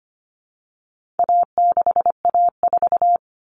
Morse code, heard as a keyed tone, about 25 wpm, 700 Hz. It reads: A6A4
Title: Common Cut Numbers - 4-Digit 25wpm